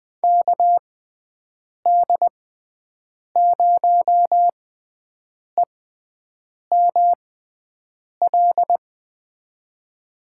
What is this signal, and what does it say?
Morse code, heard as a keyed tone, about 20 wpm, spaced as for 7 wpm, 700 Hz. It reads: KD0EML
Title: USA Call Signs - 6x Character Spacing - 20wpm